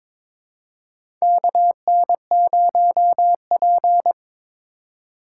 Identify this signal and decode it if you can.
Morse code, heard as a keyed tone, about 21 wpm, 700 Hz. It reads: KN0P
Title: USA Call Signs - 4 Characters - 3x Character Spacing - 22wpm